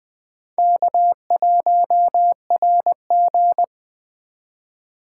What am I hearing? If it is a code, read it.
Morse code, heard as a keyed tone, about 20 wpm, 700 Hz. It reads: K1RG
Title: Speed Racing - USA Call Signs 17wpm